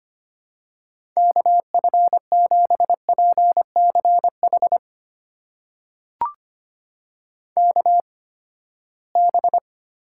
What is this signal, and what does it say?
Morse code, heard as a keyed tone, about 25 wpm, spaced as for 17 wpm, 700 Hz. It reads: KF7PCH KB
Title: USA Call Signs - 6 Characters - 8x Character Spacing - 25wpm